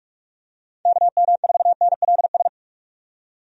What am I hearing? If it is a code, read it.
Morse code, heard as a keyed tone, about 47 wpm, 700 Hz. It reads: KM4NLS
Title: USA Call Signs - 6 Characters - 45wpm